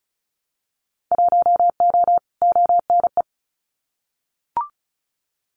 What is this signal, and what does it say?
Morse code, heard as a keyed tone, about 34 wpm, 700 Hz. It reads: 1O ONE
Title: Winter Field Day Exchanges 35wpm